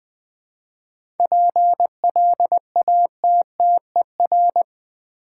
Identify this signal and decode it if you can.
Morse code, heard as a keyed tone, about 20 wpm, 700 Hz. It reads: PLATTER
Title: Easy Words with Hints - 20wpm